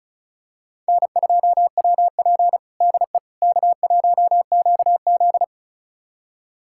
Morse code, heard as a keyed tone, about 34 wpm, 700 Hz. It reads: N2WP DE K1QZ
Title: USA Callsign DE Callsign - 4 Characters - 35wpm